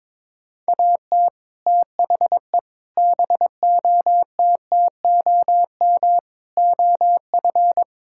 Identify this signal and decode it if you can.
Morse code, heard as a keyed tone, about 22 wpm, 700 Hz. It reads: AT THE BOTTOM OF
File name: practice / 22wpm / Build A Phrase - 5-Word Phrases - 22wpm